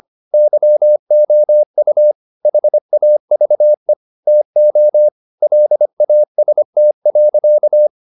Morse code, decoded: YOU HAVE TO LAST.

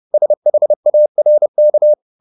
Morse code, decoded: SHARK

